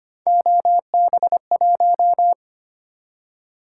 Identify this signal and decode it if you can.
Morse code, heard as a keyed tone, about 25 wpm, 700 Hz. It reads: OB1
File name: Three Letter-Number Combinations - 25wpm